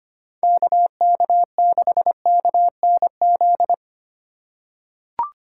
Morse code, heard as a keyed tone, about 25 wpm, 700 Hz. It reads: KK6KNZ
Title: USA Call Signs - 6 Characters - 4x Character Spacing - 25wpm